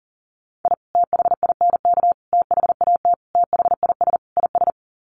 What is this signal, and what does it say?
Morse code, heard as a keyed tone, about 40 wpm, 700 Hz. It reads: I THINK THAT THIS IS
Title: Sets of 5 Words Encore2 40wpm